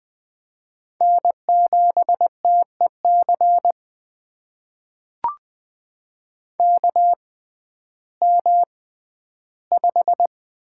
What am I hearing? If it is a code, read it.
Morse code, heard as a keyed tone, about 20 wpm, spaced as for 12 wpm, 700 Hz. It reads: N7TEC KM5
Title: USA Call Signs - 5 Characters - 6x Character Spacing - 20wpm